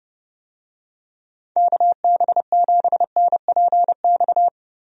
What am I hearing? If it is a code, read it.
Morse code, heard as a keyed tone, about 30 wpm, 700 Hz. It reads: KB7NPX
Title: USA Call Signs - 6 Characters - Repeated 3x - 30wpm